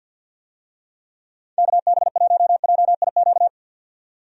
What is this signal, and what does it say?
Morse code, heard as a keyed tone, about 51 wpm, 700 Hz. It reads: KB1JIQ
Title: USA Call Signs - 6 Characters - 50wpm